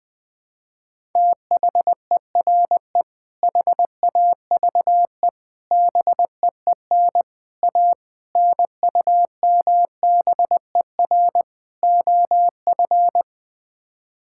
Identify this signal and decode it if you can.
Morse code, heard as a keyed tone, about 20 wpm, 700 Hz. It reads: THERE HAVE BEEN A NUMBER OF
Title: Sets of 6 Words - 20wpm